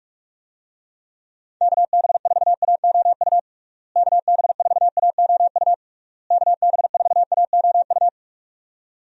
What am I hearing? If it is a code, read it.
Morse code, heard as a keyed tone, about 46 wpm, 700 Hz. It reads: KB4AOU KB4AOU KB4AOU
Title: USA Call Signs - 6 Characters - Repeated 3x - 45wpm